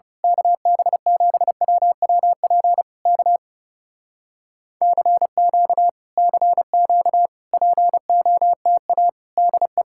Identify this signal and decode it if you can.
Morse code, heard as a keyed tone, about 28 wpm, 700 Hz. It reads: KB7WWP K CQ CQ POTA DE
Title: Speed-Racing Common QSO Phrases - 30wpm